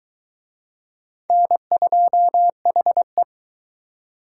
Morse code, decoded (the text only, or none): N2HE